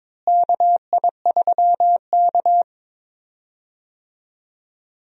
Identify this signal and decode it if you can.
Morse code, heard as a keyed tone, about 22 wpm, 700 Hz. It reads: KI3K
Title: USA Call Signs - 4 Characters - Repeated 3x - 22wpm